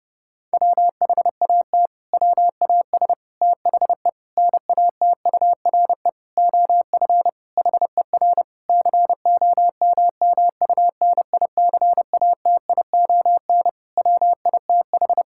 Morse code, heard as a keyed tone, about 28 wpm, 700 Hz. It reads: WHAT WAS THE NATURE OF HER COMMUNICATION WITH
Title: Sentences from Top 1000 Words - Review Part 6 - 30wpm